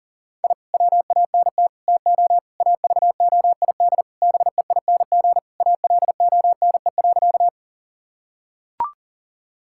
Morse code, heard as a keyed tone, about 40 wpm, 700 Hz. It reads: I WANT TO AVOID BEING ALONE.
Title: Sentences from Top 1000 Words - Review Part 6 - 40wpm